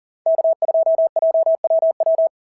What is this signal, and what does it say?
Morse code, heard as a keyed tone, about 40 wpm, 690 Hz. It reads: K2JWW
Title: SOTA QSO Practice 40wpm